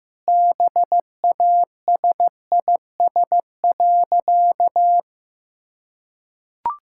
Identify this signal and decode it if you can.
Morse code, heard as a keyed tone, about 15 wpm, 700 Hz. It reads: BASIS.
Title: Sentences from Top 1000 Words - Review Part 5 - 15wpm